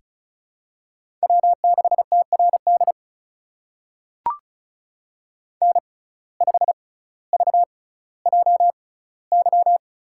Morse code, heard as a keyed tone, about 36 wpm, spaced as for 18 wpm, 700 Hz. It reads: W6TRD N5VJY
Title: USA Call Signs - 5 Characters - 6x Character Spacing - 35wpm